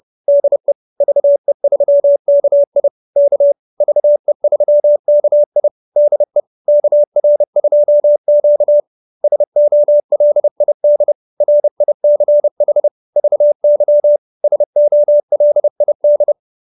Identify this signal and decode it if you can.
Morse code, heard as a keyed tone, about 30 wpm, 590 Hz. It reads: DE VE3KI K VE3KI DE KR2Q SOLID RICH VY SOLID